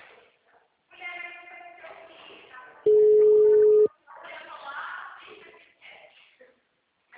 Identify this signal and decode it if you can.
Ringback tone (Europe, 425 Hz), with speech.
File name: out- -2018- - - .7970